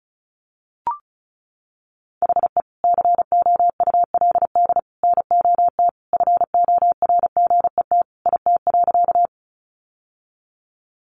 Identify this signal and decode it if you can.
Morse code, heard as a keyed tone, about 34 wpm, 700 Hz. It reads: HE COULD NOT FORGET IT.
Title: Sentences from Top 1200 Words 35wpm